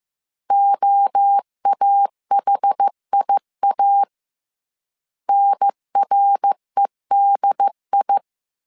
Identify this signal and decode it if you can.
Morse code, heard as a keyed tone, about 15 wpm, 800 Hz. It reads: OAHIA NREDI